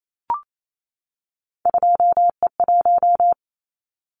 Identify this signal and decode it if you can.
Morse code, heard as a keyed tone, about 28 wpm, 700 Hz. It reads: 2E1
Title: Callsign Prefixes for Top 25 DXCC - 28wpm